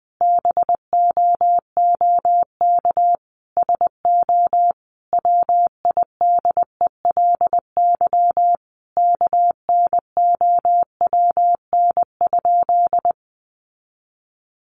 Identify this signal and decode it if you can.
Morse code, heard as a keyed tone, about 20 wpm, 700 Hz. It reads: BOOK SO WIDELY KNOWN?
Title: Sentences from Top 1200 Words 20wpm